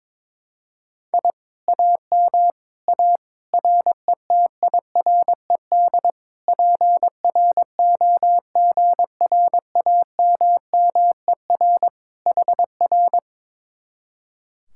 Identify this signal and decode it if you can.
Morse code, heard as a keyed tone, about 22 wpm, 700 Hz. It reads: I AM A RETIRED PROGRAMMER HR
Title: Speed-Racing Common QSO Phrases - 22wpm